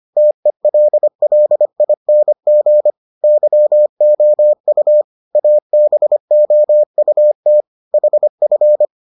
Morse code, heard as a keyed tone, about 25 wpm, 600 Hz. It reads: TELLING YOU ABOUT HF